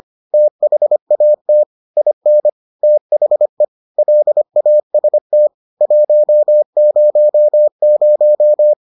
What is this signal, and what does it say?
Morse code, heard as a keyed tone, about 25 wpm, 600 Hz. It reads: THAT IN THE LAST 100